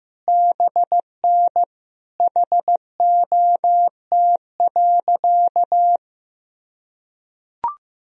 Morse code, heard as a keyed tone, about 15 wpm, 700 Hz. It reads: BN HOT.
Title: Sentences from Top 500 Words and Abbreviations 15wpm